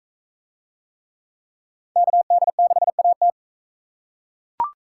Morse code, heard as a keyed tone, about 43 wpm, 700 Hz. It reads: KD6AT